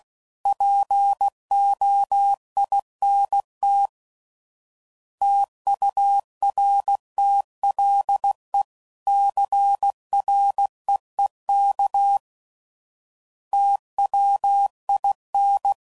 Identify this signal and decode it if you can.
Morse code, heard as a keyed tone, about 16 wpm, 800 Hz. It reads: POINT TURTLE CREEK TWIN
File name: CW Code Groups @15WPM / fromSM's / Cities